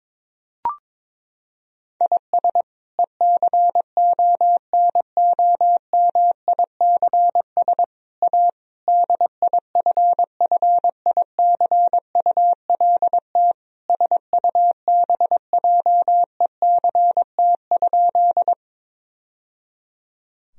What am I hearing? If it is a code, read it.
Morse code, heard as a keyed tone, about 22 wpm, 700 Hz. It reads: IS ECONOMICS A DIFFICULT SUBJECT?
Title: Sentences from Top 1000 Words - Review Part 3 - 22wpm